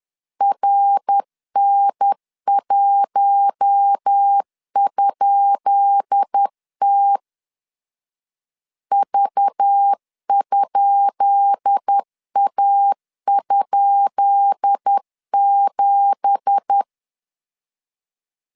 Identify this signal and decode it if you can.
Morse code, heard as a keyed tone, about 10 wpm, 800 Hz. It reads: RN1?T V?A?7
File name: LEZIONI / 10 SIGNS / 10B INT 10wpm 2844